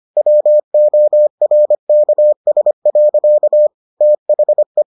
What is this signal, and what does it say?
Morse code, heard as a keyed tone, about 25 wpm, 600 Hz. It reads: WORKS. THE